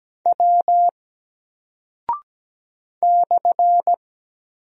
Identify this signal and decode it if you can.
Morse code, heard as a keyed tone, about 17 wpm, 700 Hz. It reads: W /
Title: Single Character (Letter:Number:Punctuation:Prosign) 17wpm